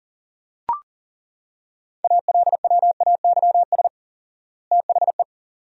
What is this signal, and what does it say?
Morse code, heard as a keyed tone, about 40 wpm, 700 Hz. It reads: ALWAYS THE
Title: 4x Word-Spacing - Sentences from Top 300 Words - 40wpm